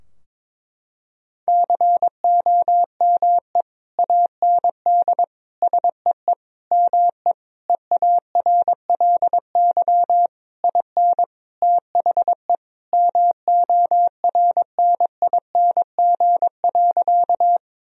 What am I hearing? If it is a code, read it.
Morse code, heard as a keyed tone, about 22 wpm, 700 Hz. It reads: COME AND SEE ME EARLY IN THE MORNING.
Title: Sentences from Top 1000 Words - Review Part 2 - 22wpm